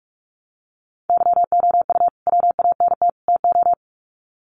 Morse code, heard as a keyed tone, about 44 wpm, 700 Hz. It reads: YOU WANT TO
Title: Sets of 3 Words 45wpm